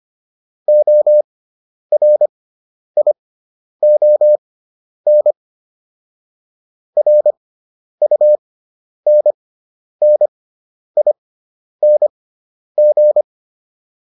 Morse code, heard as a keyed tone, about 25 wpm, spaced as for 10 wpm, 600 Hz. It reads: ORION RUNNING